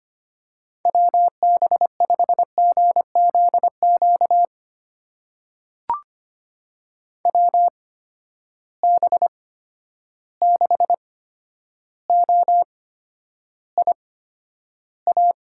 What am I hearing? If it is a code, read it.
Morse code, heard as a keyed tone, about 25 wpm, spaced as for 11 wpm, 700 Hz. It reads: WB5GZQ WB6OIA
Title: USA Call Signs - 6 Characters - 8x Character Spacing - 25wpm